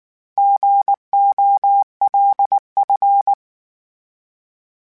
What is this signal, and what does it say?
Morse code, heard as a keyed tone, about 19 wpm, 800 Hz. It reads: GOLF